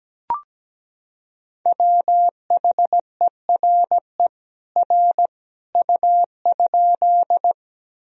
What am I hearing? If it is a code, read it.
Morse code, heard as a keyed tone, about 17 wpm, 700 Hz. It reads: WHERE R U?
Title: Sentences from Top 200 Words and Abbreviations 17wpm